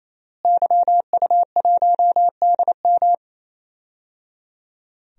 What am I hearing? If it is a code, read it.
Morse code, heard as a keyed tone, about 28 wpm, 700 Hz. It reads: YU1DM
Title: DXCC-100 Call Signs - 5 Characters - Repeated 3x - 28wpm